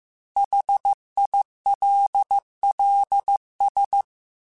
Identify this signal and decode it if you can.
Morse code, heard as a keyed tone, about 15 wpm, 800 Hz. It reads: HILLS